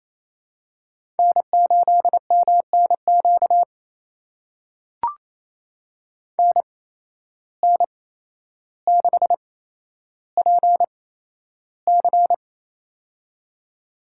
Morse code, heard as a keyed tone, about 28 wpm, spaced as for 12 wpm, 700 Hz. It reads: N8MNQ NN6PC
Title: USA Call Signs - 5 Characters - 8x Character Spacing - 28wpm